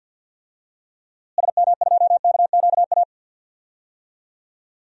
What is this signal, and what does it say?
Morse code, heard as a keyed tone, about 46 wpm, 700 Hz. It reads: SM2KQA
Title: DXCC-100 Call Signs - 6 Characters - Repeated 3x - 50wpm